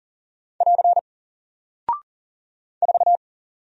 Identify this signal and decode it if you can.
Morse code, heard as a keyed tone, about 40 wpm, 700 Hz. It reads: <AR> 4
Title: Single Character (Letter:Number:Punctuation:Prosign) Alt 40wpm